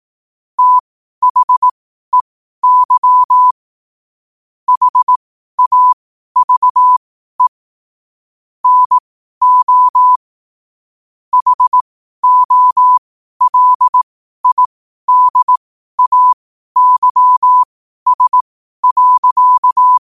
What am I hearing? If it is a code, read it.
Morse code, heard as a keyed tone, about 18 wpm, spaced as for 12 wpm, 1000 Hz. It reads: THEY HAVE NO HOLIDAYS.
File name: qotdfiles / 12wpm